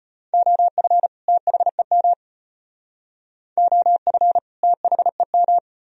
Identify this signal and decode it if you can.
Morse code, heard as a keyed tone, about 32 wpm, 700 Hz. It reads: OF THEM OF THEM